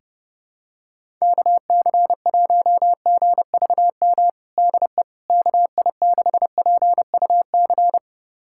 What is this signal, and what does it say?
Morse code, heard as a keyed tone, about 32 wpm, 700 Hz. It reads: KC1GVM DE KI6PUC
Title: USA Callsign DE Callsign 30wpm